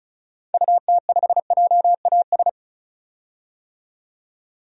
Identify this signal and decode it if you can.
Morse code, heard as a keyed tone, about 34 wpm, 700 Hz. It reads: UT5JAS
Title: DXCC-100 Call Signs - 6 Characters - 35wpm